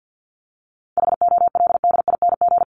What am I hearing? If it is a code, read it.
Morse code, heard as a keyed tone, about 49 wpm, 700 Hz. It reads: HOLDING
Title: ING Suffix - Top 100 Words with Base Words 50wpm